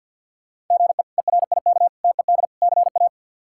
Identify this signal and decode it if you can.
Morse code, heard as a keyed tone, about 53 wpm, 690 Hz. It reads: GE ERIK TED CA
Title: K1USN SST QSO Practice - 50wpm